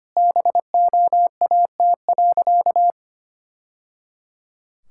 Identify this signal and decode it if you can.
Morse code, heard as a keyed tone, about 25 wpm, 700 Hz. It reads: BOAT.
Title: Sentences from Top 1000 Words - Review Part 6 - 25wpm